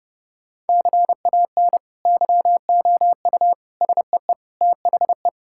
Morse code, heard as a keyed tone, about 32 wpm, 700 Hz. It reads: CAN YOU SEE THE
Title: Sentences from Top 1000 Words - Review Part 5 - 30wpm